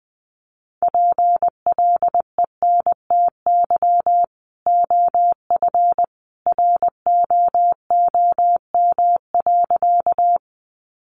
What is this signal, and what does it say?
Morse code, heard as a keyed tone, about 20 wpm, 700 Hz. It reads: PLENTY OF ROOM.
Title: Sentences from Top 1300 Words 20wpm